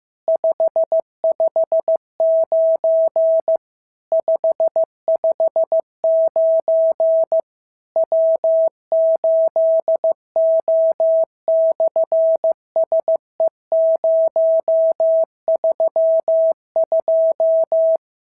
Morse code, heard as a keyed tone, about 15 wpm, 650 Hz. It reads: 559 559 W8O/SE032
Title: SOTA QSO Practice 15wpm